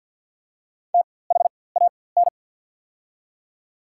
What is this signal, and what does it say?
Morse code, heard as a keyed tone, about 48 wpm, spaced as for 23 wpm, 700 Hz. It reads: THAN
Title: 4x Character Spacing - Top 300 Words - 50wpm